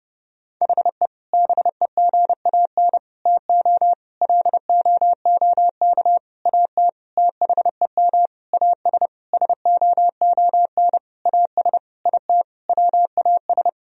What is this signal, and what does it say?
Morse code, heard as a keyed tone, about 30 wpm, 700 Hz. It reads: HE BEGAN TO LOOK AT THEM AS SOON AS IT WAS
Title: Sentences from Top 1000 Words - Review Part 1 - 30wpm